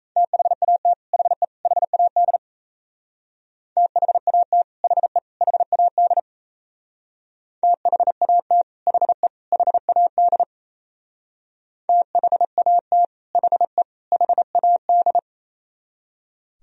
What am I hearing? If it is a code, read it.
Morse code, heard as a keyed tone, about 35 wpm, 700 Hz. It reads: THAT HE HAD THAT HE HAD THAT HE HAD THAT HE HAD